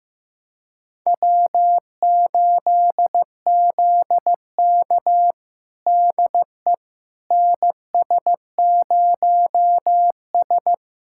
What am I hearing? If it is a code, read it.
Morse code, heard as a keyed tone, about 15 wpm, 700 Hz. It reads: W8ZK DE NS0S
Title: USA Callsign DE Callsign - 4 Characters - 15wpm